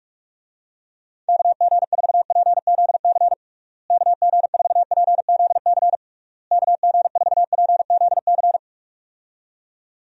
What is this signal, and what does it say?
Morse code, heard as a keyed tone, about 41 wpm, 700 Hz. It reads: KG4PZC KG4PZC KG4PZC
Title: USA Call Signs - 6 Characters - Repeated 3x - 45wpm